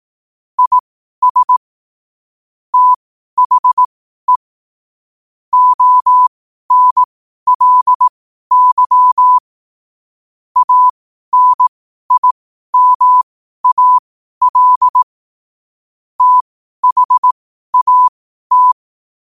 Morse code, heard as a keyed tone, about 18 wpm, spaced as for 12 wpm, 1000 Hz. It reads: IS THE ONLY ANIMAL THAT
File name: qotdfiles / 12wpm